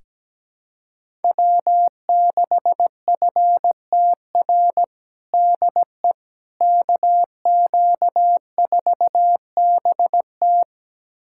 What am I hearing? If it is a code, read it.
Morse code, heard as a keyed tone, about 17 wpm, 700 Hz. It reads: W6FTR DE KQ4BT
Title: USA Callsign DE Callsign - 5 Characters - 17wpm